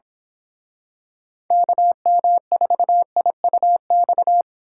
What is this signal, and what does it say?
Morse code, heard as a keyed tone, about 28 wpm, 700 Hz. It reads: KM4IUX